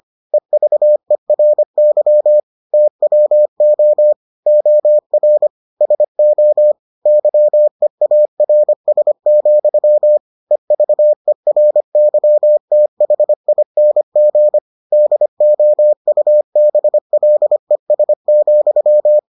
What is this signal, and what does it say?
Morse code, heard as a keyed tone, about 25 wpm, 600 Hz. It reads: EVERY TWO OR SO YEARS, EVERYTHING DOUBLES,